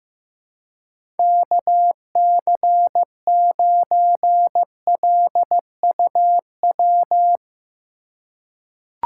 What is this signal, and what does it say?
Morse code, heard as a keyed tone, about 15 wpm, 700 Hz. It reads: KC9LUW
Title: USA Call Signs - 6 Characters - 4x Character Spacing - 15wpm